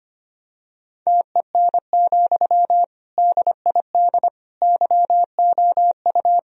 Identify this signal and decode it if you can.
Morse code, heard as a keyed tone, about 25 wpm, 700 Hz. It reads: TEN, DID YOU
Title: Sentences from Top 1000 Words - Review Part 2 - 25wpm